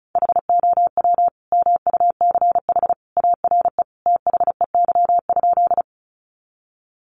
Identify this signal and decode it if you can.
Morse code, heard as a keyed tone, about 34 wpm, 700 Hz. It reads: HOW MUCH ARE THEY?
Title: Sentences from Top 200 Words 35wpm